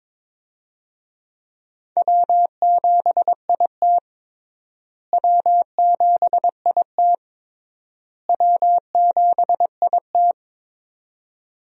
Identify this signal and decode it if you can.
Morse code, heard as a keyed tone, about 22 wpm, 700 Hz. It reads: W7IT W7IT W7IT
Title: USA Call Signs - 4 Characters - Repeated 3x - 22wpm